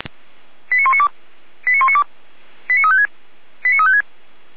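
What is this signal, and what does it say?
Five-tone selective call (ZVEI), heard as quick runs of five tones, tones of about 70 ms. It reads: 89192 89192 89356 89356
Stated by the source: ZVEI five-tone selective call received on SDR.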